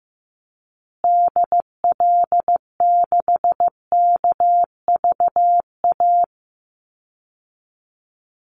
Morse code, heard as a keyed tone, about 15 wpm, 700 Hz. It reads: DL6KVA
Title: Call Signs Repeated 2x - 15wpm